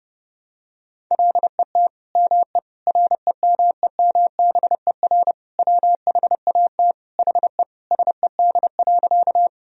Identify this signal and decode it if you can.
Morse code, heard as a keyed tone, about 30 wpm, 700 Hz. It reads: LET ME REMEMBER WHAT HE SED.
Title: Sentences from Top 400 Words and Abbreviations 30wpm